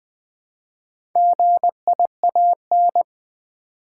Morse code, heard as a keyed tone, about 20 wpm, 700 Hz. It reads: GIAN
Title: Build A Word - 1 Letter At A Time - 5-Letter Words - 20wpm